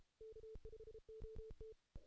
Morse code, heard as a keyed tone, about 33 wpm, 440 Hz. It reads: K5OT
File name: 2016 JAN CW NAQP